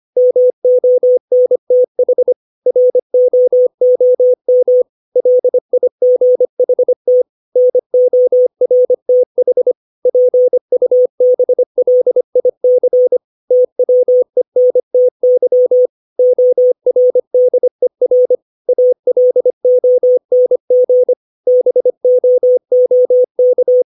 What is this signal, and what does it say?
Morse code, heard as a keyed tone, about 25 wpm, 500 Hz. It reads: MONTH ROOM LIGHT NORTH PUBLIC TWENTY ORDER ALONG BOOK
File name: 632 words / 632 Words 25WPM